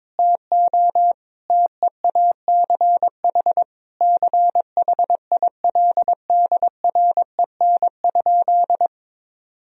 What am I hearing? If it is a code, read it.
Morse code, heard as a keyed tone, about 22 wpm, 700 Hz. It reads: TO TEACH CHILDREN?
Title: Sentences from Top 1000 Words - Review Part 2 - 22wpm